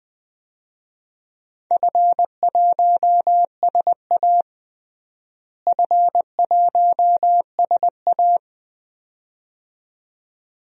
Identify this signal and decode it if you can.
Morse code, heard as a keyed tone, about 20 wpm, 700 Hz. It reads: F1SA F1SA
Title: DXCC-100 Call Signs - 4 Characters - Repeated 2x - 20wpm